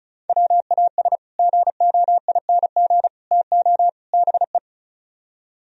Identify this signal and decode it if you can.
Morse code, heard as a keyed tone, about 42 wpm, 700 Hz. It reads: WAS GOING TO BE
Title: Speed-Racing - Sets of 4 Words 35wpm